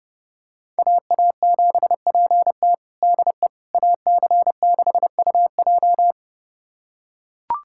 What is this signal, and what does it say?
Morse code, heard as a keyed tone, about 32 wpm, 700 Hz. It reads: AA7PT DE AC6UJ
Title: USA Callsign DE Callsign - 5 Characters - 30wpm